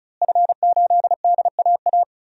Morse code, heard as a keyed tone, about 34 wpm, 700 Hz. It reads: F8DAA
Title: DXCC-100 Call Signs - 5 Characters - Repeated 3x - 35wpm